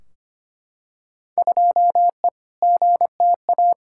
Morse code, heard as a keyed tone, about 25 wpm, 700 Hz. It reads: 2E GTA
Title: Field Day Exchanges Encore - 25wpm